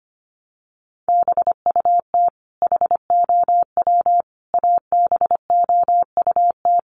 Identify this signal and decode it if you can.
Morse code, heard as a keyed tone, about 25 wpm, 700 Hz. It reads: BUT HOW ABOUT
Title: Sentences from Top 100 Words 25wpm